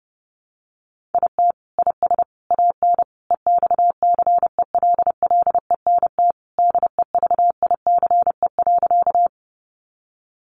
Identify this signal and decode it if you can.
Morse code, heard as a keyed tone, about 30 wpm, 700 Hz. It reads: IT IS AN EXCELLENT DEVICE.
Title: Sentences from Top 1500 Words 30wpm